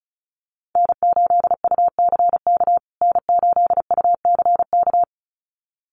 Morse code, heard as a keyed tone, about 34 wpm, 700 Hz. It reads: N8UCK N8UCK
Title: USA Call Signs Repeated 2x 35wpm